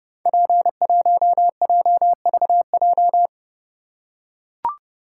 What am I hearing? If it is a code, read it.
Morse code, heard as a keyed tone, about 30 wpm, 700 Hz. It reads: P1JVJ
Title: Code Groups - 5 Letter:Numbers - 30wpm